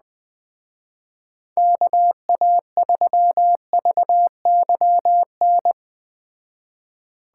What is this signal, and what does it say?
Morse code, heard as a keyed tone, about 20 wpm, 700 Hz. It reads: KA3VYN